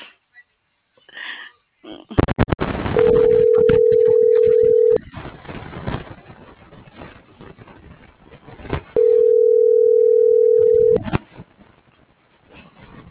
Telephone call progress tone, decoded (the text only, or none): ringback tone